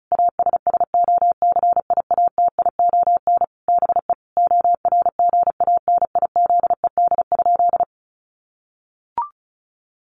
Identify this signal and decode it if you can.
Morse code, heard as a keyed tone, about 34 wpm, 700 Hz. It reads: ASSOCIATION BE ORGANIZED?
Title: Sentences from Top 1300 Words 35wpm